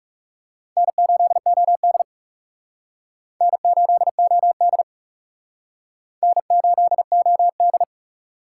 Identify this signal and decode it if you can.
Morse code, heard as a keyed tone, about 38 wpm, 700 Hz. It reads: N8OD N8OD N8OD